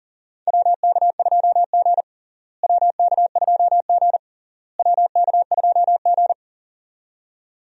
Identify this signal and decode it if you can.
Morse code, heard as a keyed tone, about 40 wpm, 700 Hz. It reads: WK2G WK2G WK2G
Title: USA Call Signs - 4 Characters - Repeated 3x - 40wpm